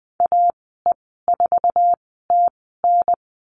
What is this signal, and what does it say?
Morse code, heard as a keyed tone, about 20 wpm, spaced as for 14 wpm, 700 Hz. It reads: AE4TN
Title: 048 Lesson 9g - Number 4 - Call Signs 20wpm